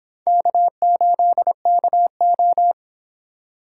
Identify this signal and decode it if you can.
Morse code, heard as a keyed tone, about 23 wpm, 700 Hz. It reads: K8KO